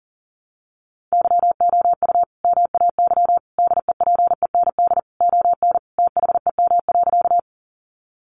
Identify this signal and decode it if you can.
Morse code, heard as a keyed tone, about 40 wpm, 700 Hz. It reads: YOU MAY DEPEND ON THEM.
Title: Sentences from Top 1300 Words 40wpm